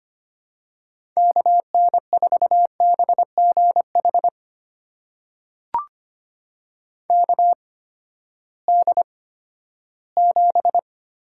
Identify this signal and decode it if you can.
Morse code, heard as a keyed tone, about 25 wpm, spaced as for 14 wpm, 700 Hz. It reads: KN4BGH KD7
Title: USA Call Signs - 6 Characters - 8x Character Spacing - 25wpm